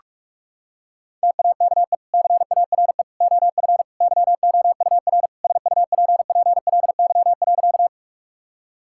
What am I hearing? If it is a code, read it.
Morse code, heard as a keyed tone, about 41 wpm, 700 Hz. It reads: TAKE CARE OF YOUR SUPPLY.
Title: Sentences from Top 1000 Words - Review Part 5 - 45wpm